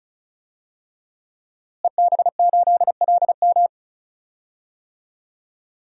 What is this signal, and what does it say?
Morse code, heard as a keyed tone, about 34 wpm, 700 Hz. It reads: EB8LM
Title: DXCC-100 Call Signs - 5 Characters - Repeated 3x - 35wpm